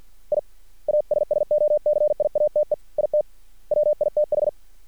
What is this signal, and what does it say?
Morse code, heard as a keyed tone, about 50 wpm, 600 Hz. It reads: I ASSOCIATE IT WITH
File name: archived / CRSnet